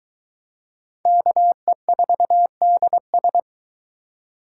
Morse code, heard as a keyed tone, about 22 wpm, 700 Hz. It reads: KE4DS